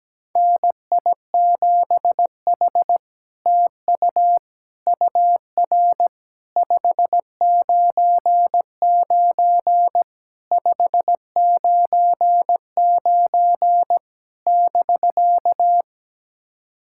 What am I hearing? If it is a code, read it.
Morse code, heard as a keyed tone, about 17 wpm, 700 Hz. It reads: NI7H TU UR 599 599 <BK>
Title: SOTA QSO Practice 17wpm